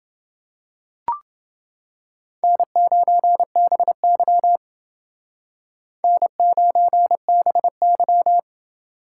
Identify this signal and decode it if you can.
Morse code, heard as a keyed tone, about 30 wpm, 700 Hz. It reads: N9BY N9BY